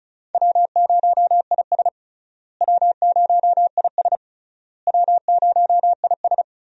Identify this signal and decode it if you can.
Morse code, heard as a keyed tone, about 36 wpm, 700 Hz. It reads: W0IS W0IS W0IS
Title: USA Call Signs - 4 Characters - Repeated 3x - 35wpm